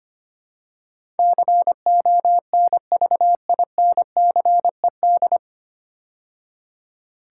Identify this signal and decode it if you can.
Morse code, heard as a keyed tone, about 25 wpm, 700 Hz. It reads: CONVINCED
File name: practice / 25wpm / 9 Letter Words - First 3 Letters Given - 25wpm